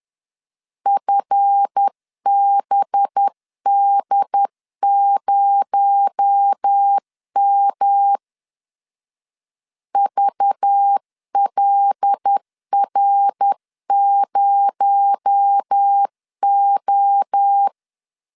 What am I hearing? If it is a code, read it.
Morse code, heard as a keyed tone, about 10 wpm, 800 Hz. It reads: FBD0M VLR0O
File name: LEZIONI / 7 GF9Z / 7 GF 10wpm 2514